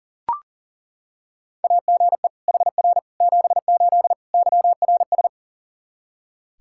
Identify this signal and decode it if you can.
Morse code, heard as a keyed tone, about 40 wpm, 700 Hz. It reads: AGE HR 78 YRS
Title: K1USN SST QSO Practice 40wpm